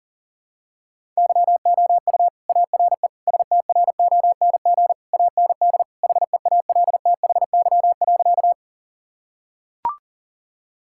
Morse code, heard as a keyed tone, about 40 wpm, 700 Hz. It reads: YOU ARE STRONG AND HEALTHY.
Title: Sentences from Top 1000 Words - Review Part 5 - 40wpm